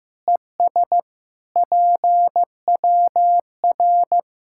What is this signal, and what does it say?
Morse code, heard as a keyed tone, about 16 wpm, 700 Hz. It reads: ES PWR
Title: K1USN SST QSO Practice 15wpm